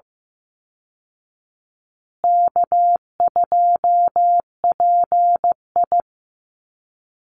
Morse code, heard as a keyed tone, about 15 wpm, 700 Hz. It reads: K2PI
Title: Easy Call Signs Repeated 3x - 15wpm